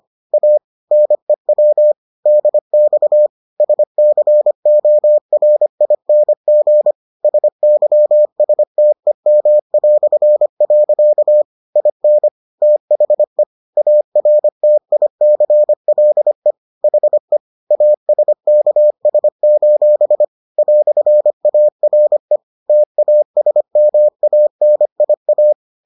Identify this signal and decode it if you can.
Morse code, heard as a keyed tone, about 25 wpm, 600 Hz. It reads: A NEW DX SCORING SYSTEM". IN THE ARTICLE HE ASKS: "ARE TASMANIA